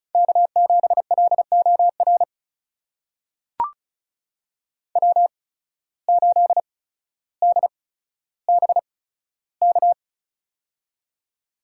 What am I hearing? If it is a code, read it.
Morse code, heard as a keyed tone, about 36 wpm, spaced as for 15 wpm, 700 Hz. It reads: K7LOR W8DBK
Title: USA Call Signs - 5 Characters - 8x Character Spacing - 35wpm